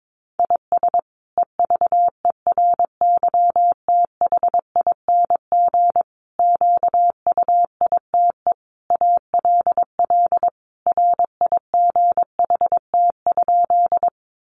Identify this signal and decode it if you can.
Morse code, heard as a keyed tone, about 22 wpm, 700 Hz. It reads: IS EVERYTHING QUITE ALL RIGHT?
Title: Sentences from Top 700 Words 22wpm